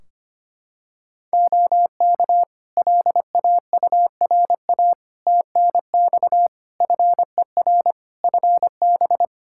Morse code, heard as a keyed tone, about 25 wpm, 700 Hz. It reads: OK LAURA TNX FER FB
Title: K1USN SST QSO Practice 25wpm